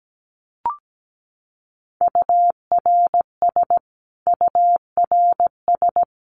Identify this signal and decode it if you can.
Morse code, heard as a keyed tone, about 17 wpm, 700 Hz. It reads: URS URS
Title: Abbreviations Repeated 3x - 17wpm